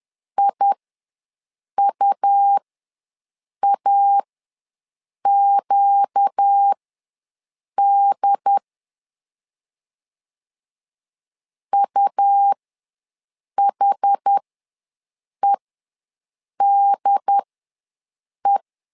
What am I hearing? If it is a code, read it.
Morse code, heard as a keyed tone, about 10 wpm, spaced as for 6 wpm, 800 Hz. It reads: IUAQD UHEDE
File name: LEZIONI / 4 Q ws 10wpm 2306